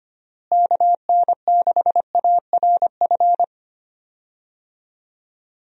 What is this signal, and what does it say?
Morse code, heard as a keyed tone, about 25 wpm, 700 Hz. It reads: KN6ARF
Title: USA Call Signs - 6 Characters - Repeated 3x - 25wpm